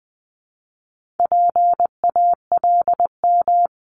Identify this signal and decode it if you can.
Morse code, heard as a keyed tone, about 20 wpm, 700 Hz. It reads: PALM
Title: Multi-letter Phonemes 20wpm